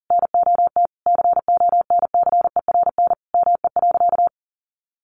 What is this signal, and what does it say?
Morse code, heard as a keyed tone, about 40 wpm, 700 Hz. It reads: NOT CONCERN ME.
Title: Sentences from Top 1000 Words 40wpm